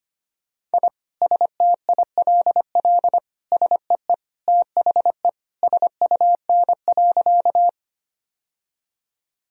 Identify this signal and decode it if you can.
Morse code, heard as a keyed tone, about 25 wpm, 700 Hz. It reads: I STILL SEE THE SUN.
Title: Sentences from Top 1000 Words - Review Part 1 - 25wpm